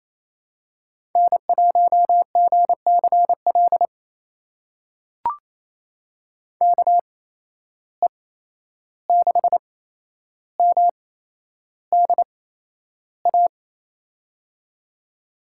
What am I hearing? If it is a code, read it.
Morse code, heard as a keyed tone, about 27 wpm, spaced as for 11 wpm, 700 Hz. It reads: N1GCL KE6MDA
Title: USA Call Signs - 8x Character Spacing - 28wpm